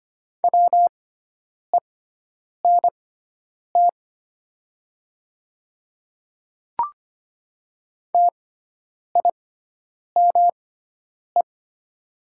Morse code, heard as a keyed tone, about 25 wpm, spaced as for 9 wpm, 700 Hz. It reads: WENT TIME
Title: 6x Character Spacing - Top 300 Words - 25wpm